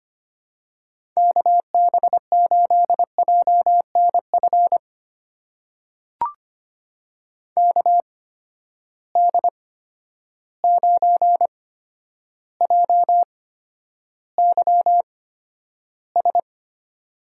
Morse code, heard as a keyed tone, about 25 wpm, spaced as for 11 wpm, 700 Hz. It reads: KB8JNF KD9JYS
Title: USA Call Signs - 6 Characters - 8x Character Spacing - 25wpm